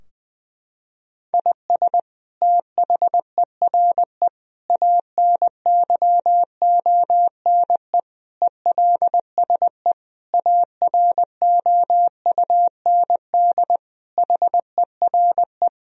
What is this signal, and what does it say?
Morse code, heard as a keyed tone, about 20 wpm, 700 Hz. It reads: IS THERE ANYONE ELSE AROUND HERE
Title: Sentences from Top 1000 Words - Review Part 3 - 20wpm